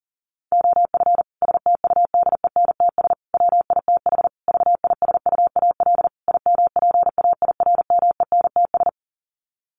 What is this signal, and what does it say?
Morse code, heard as a keyed tone, about 40 wpm, 700 Hz. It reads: OF STUDENTS WITH VISUAL IMPAIRMENTS
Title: Sets of 5 Words Encore 40wpm